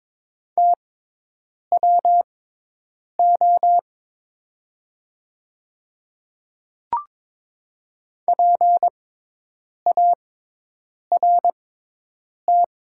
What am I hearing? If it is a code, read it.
Morse code, heard as a keyed tone, about 22 wpm, spaced as for 8 wpm, 700 Hz. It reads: TWO PART
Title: 6x Character Spacing - Top 200 Words - 22wpm